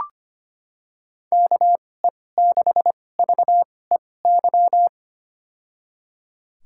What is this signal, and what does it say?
Morse code, heard as a keyed tone, about 25 wpm, spaced as for 18 wpm, 700 Hz. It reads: KE6VEY
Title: USA Call Signs - 2x Character Spacing - 25wpm